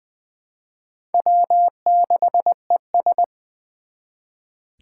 Morse code, heard as a keyed tone, about 20 wpm, 700 Hz. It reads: W6ES
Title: Four Challenging Letter-Number Combinations 20wpm